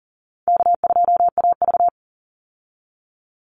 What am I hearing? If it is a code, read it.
Morse code, heard as a keyed tone, about 40 wpm, 700 Hz. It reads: K2AV
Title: Call Signs - 40wpm